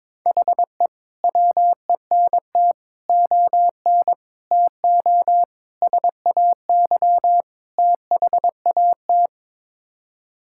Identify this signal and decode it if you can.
Morse code, heard as a keyed tone, about 21 wpm, 700 Hz. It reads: HE WENT ON TO SAY THAT
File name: practice / 22wpm / Sets of 6 Words - 22wpm